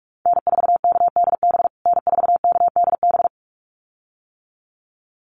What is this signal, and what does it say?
Morse code, heard as a keyed tone, about 44 wpm, 700 Hz. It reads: N4KDB N4KDB
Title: USA Call Signs Repeated 2x 45wpm